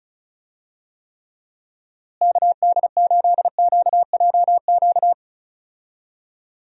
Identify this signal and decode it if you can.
Morse code, heard as a keyed tone, about 34 wpm, 700 Hz. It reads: KD8QJQ
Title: USA Call Signs - 6 Characters - Repeated 2x - 35wpm